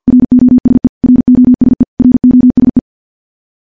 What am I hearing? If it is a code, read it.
Morse code, heard as a keyed tone, about 50 wpm, 250 Hz. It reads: ROSE ROSE ROSE
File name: morse / morse-training-50WPM